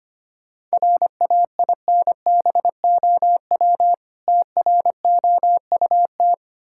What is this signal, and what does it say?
Morse code, heard as a keyed tone, about 25 wpm, 700 Hz. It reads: RAINBOW TROUT
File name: practice / 25wpm / Difficult Words with Hints - 25wpm